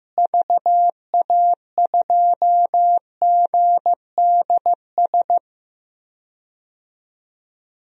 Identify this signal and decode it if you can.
Morse code, heard as a keyed tone, about 15 wpm, 700 Hz. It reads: VA2GDS
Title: DXCC-100 Call Signs - 6 Characters - Repeated 2x - 15wpm